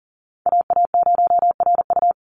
Morse code, heard as a keyed tone, about 40 wpm, 690 Hz. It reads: AA0RU
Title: USA Call Signs 40wpm